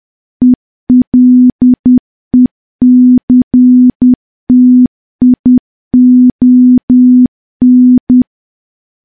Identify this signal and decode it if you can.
Morse code, heard as a keyed tone, about 10 wpm, 250 Hz. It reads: ELECTION
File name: morse / morse-training-10WPM